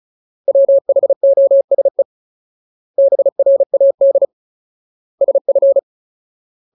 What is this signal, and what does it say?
Morse code, heard as a keyed tone, about 35 wpm, 550 Hz. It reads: WHOSE BRAD SF
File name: Ultimate Word List / 35WPM Ultimate